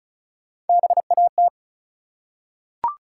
Morse code, heard as a keyed tone, about 36 wpm, 700 Hz. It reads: BAT
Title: Three Letter Words - 35wpm